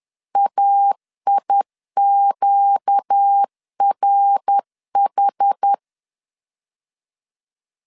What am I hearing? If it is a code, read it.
Morse code, heard as a keyed tone, about 10 wpm, 800 Hz. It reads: AIQRH
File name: LEZIONI / 6 PKX0 / 6 PK 10wpm 2436